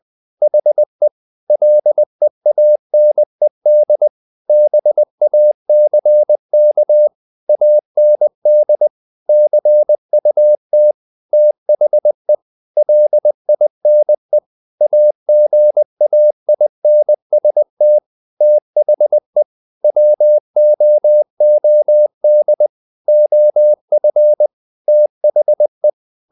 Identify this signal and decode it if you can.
Morse code, heard as a keyed tone, about 20 wpm, 600 Hz. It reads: HE LEANED BACK AND CUT THE LINE AGAINST THE WOOD OF THE
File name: the old man and the sea / the old man and the sea part 1 20wpm